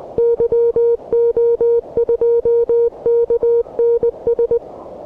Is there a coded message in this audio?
Yes: Morse code from 0.2 to 4.6 s, heard as a keyed tone, about 20 wpm, 470 Hz. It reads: YO2KNS